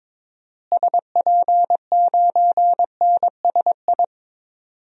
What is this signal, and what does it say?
Morse code, heard as a keyed tone, about 22 wpm, 700 Hz. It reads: SP9NSI
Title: DXCC-100 Call Signs - 6 Characters - Repeated 2x - 22wpm